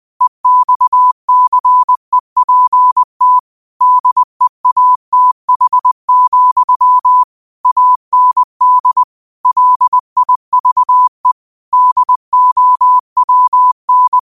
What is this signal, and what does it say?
Morse code, heard as a keyed tone, about 20 wpm, 1000 Hz. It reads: EXCEPT DEATH, AND LIVE DOWN
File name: qotdfiles / 20wpm